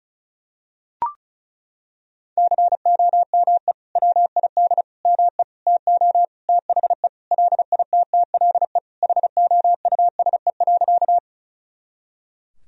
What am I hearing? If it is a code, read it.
Morse code, heard as a keyed tone, about 34 wpm, 700 Hz. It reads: COME WID ME TO THE LITTLE HOUSE.
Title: Sentences from Top 200 Words and Abbreviations 35wpm